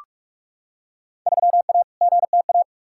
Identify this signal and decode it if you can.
Morse code, heard as a keyed tone, about 45 wpm, 700 Hz. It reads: 3A GTA
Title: Field Day Exchanges - 45wpm